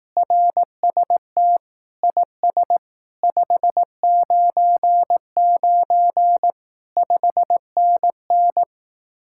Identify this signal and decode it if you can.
Morse code, heard as a keyed tone, about 18 wpm, 700 Hz. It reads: RST IS 599 5NN
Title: Speed-Racing Common QSO Phrases Encore - 15wpm